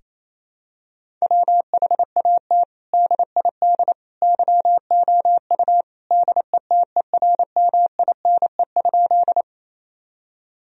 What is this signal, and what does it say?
Morse code, heard as a keyed tone, about 28 wpm, 700 Hz. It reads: WHAT DID YOU DETERMINE?
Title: Sentences from Top 1000 Words - Review Part 4 - 28wpm